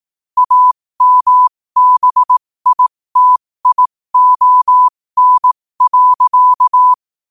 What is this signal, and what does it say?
Morse code, heard as a keyed tone, about 18 wpm, spaced as for 15 wpm, 1000 Hz. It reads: AMBITION.
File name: qotdfiles / 15wpm